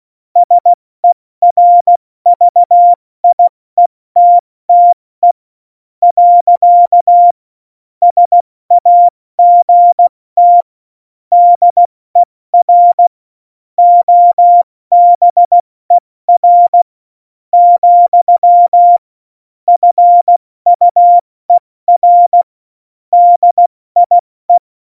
Morse code, heard as a keyed tone, about 16 wpm, spaced as for 13 wpm, 700 Hz. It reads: SERVIETTE. SAGT DER OBER, FUER DIE
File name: Morsekurs / Witze / Witz 09. April 2019, 70 F80